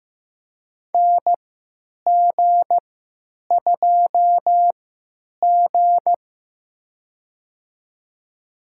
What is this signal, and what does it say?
Morse code, heard as a keyed tone, about 15 wpm, spaced as for 9 wpm, 700 Hz. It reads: NG2G
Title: USA Call Signs - 4 Characters - 3x Character Spacing - 15wpm